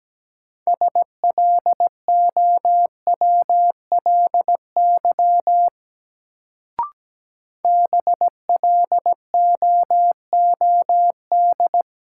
Morse code, heard as a keyed tone, about 17 wpm, 700 Hz. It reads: SLOWLY BLOOD
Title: Sentences from Top 1100 Words - New Word Drill Alt - 17wpm